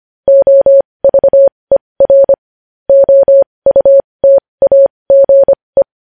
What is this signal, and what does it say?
Morse code, heard as a keyed tone, about 25 wpm, spaced as for 20 wpm, 550 Hz. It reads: OVER OUTAGE
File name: podcast / News headlines-1x-25x20